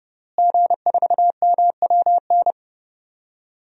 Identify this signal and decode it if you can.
Morse code, heard as a keyed tone, about 29 wpm, 700 Hz. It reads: G4MWN
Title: DXCC-100 Call Signs - 5 Characters - Repeated 3x - 30wpm